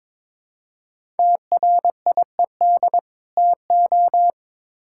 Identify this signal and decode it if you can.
Morse code, heard as a keyed tone, about 22 wpm, 700 Hz. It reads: TRIED TO
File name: Build A Phrase - 2-Word Phrases - 22wpm